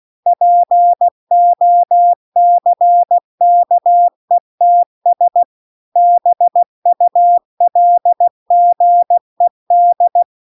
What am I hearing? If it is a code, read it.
Morse code, heard as a keyed tone, about 16 wpm, 700 Hz. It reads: POCKETS BULGED